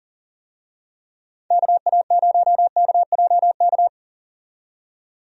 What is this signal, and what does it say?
Morse code, heard as a keyed tone, about 40 wpm, 700 Hz. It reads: KA0KJK
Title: USA Call Signs - 6 Characters - 40wpm